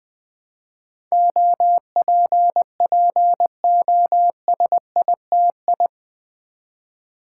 Morse code, heard as a keyed tone, about 20 wpm, 700 Hz. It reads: OPPOSITI
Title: Build A Word - 4 Letters At A Time - 10-Letter Words - 20wpm